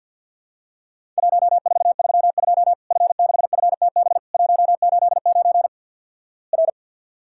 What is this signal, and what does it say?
Morse code, heard as a keyed tone, about 50 wpm, 700 Hz. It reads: 1432 F6FTB 189 R
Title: Worked All Europe DX Contest - QTC Exchange - 50wpm